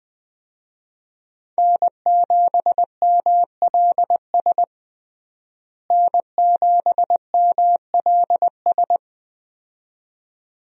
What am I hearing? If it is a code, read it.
Morse code, heard as a keyed tone, about 20 wpm, 700 Hz. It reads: N7MLS N7MLS
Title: USA Call Signs - 5 Characters - Repeated 2x - 20wpm